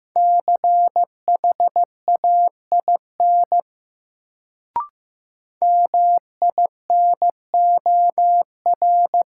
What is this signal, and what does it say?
Morse code, heard as a keyed tone, about 16 wpm, 700 Hz. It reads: CHAIN MINOR
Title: Sentences from Top 1500 Words - New Word Drill - 15wpm